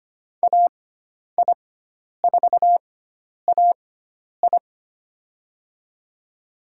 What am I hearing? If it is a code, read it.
Morse code, heard as a keyed tone, about 25 wpm, spaced as for 10 wpm, 700 Hz. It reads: AI4AI
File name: Lesson 16g - Letter c - Call Signs 25wpm